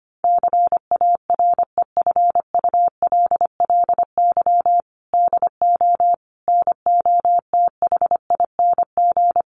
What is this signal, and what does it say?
Morse code, heard as a keyed tone, about 25 wpm, 700 Hz. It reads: CAREFULLY DO NOTHING
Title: Sentences from Top 1300 Words 25wpm